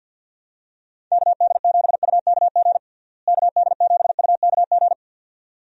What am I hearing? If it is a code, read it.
Morse code, heard as a keyed tone, about 52 wpm, 700 Hz. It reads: KD7UKG KD7UKG
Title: USA Call Signs - 6 Characters - Repeated 2x - 50wpm